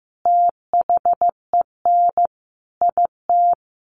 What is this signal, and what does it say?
Morse code, heard as a keyed tone, about 15 wpm, 700 Hz. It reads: THEN IT
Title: Sentences from Top 1400 Words 15wpm